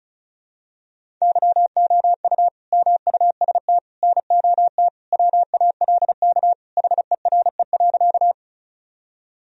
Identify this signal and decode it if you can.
Morse code, heard as a keyed tone, about 35 wpm, 700 Hz. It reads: YOU MUST NOT WALK HERE.
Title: Sentences from Top 1000 Words - Review Part 1 - 35wpm